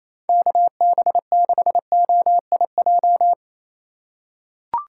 Morse code, heard as a keyed tone, about 28 wpm, 700 Hz. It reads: KB6OIJ
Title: USA Call Signs - 6 Characters - 2x Character Spacing - 28wpm